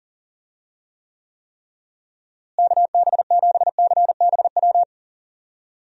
Morse code, heard as a keyed tone, about 37 wpm, 700 Hz. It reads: KB7CBW
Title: USA Call Signs - 6 Characters - Repeated 3x - 40wpm